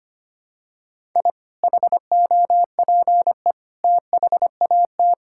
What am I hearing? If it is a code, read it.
Morse code, heard as a keyed tone, about 25 wpm, 690 Hz. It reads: I HOPE THAT
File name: practice / 25wpm / Sentences with 2-Word Phrases - 25wpm